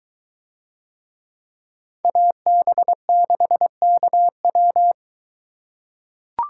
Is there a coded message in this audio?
Yes: Morse code from 2.0 to 4.9 s, heard as a keyed tone, about 22 wpm, 700 Hz. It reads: AB6KW